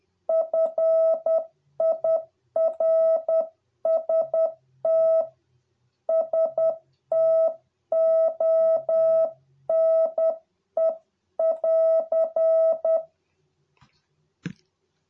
Morse code, heard as a keyed tone, about 10 wpm, 650 Hz. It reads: FIRST STONE <AR>